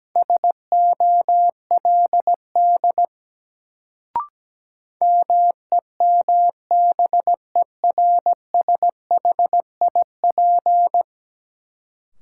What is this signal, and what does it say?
Morse code, heard as a keyed tone, about 17 wpm, 700 Hz. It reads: SOLD MEMBERSHIP
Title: Sentences from Top 1400 Words - New Word Drill - 17wpm